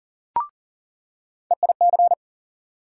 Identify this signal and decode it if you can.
Morse code, heard as a keyed tone, about 37 wpm, 700 Hz. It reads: EIC
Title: USA Call Sign Postfixes - 3 Letters 40wpm